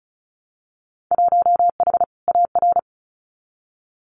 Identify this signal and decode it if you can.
Morse code, heard as a keyed tone, about 36 wpm, 700 Hz. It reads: 1H AR
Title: Winter Field Day Exchanges 35wpm